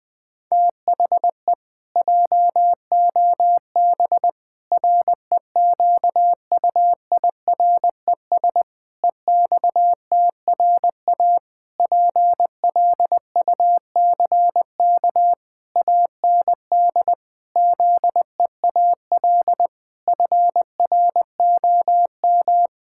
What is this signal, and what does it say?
Morse code, heard as a keyed tone, about 20 wpm, 700 Hz. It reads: THE JOB REQUIRES EXTRA PLUCK AND ZEAL FROM
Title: Pangram Challenge v2 - 20wpm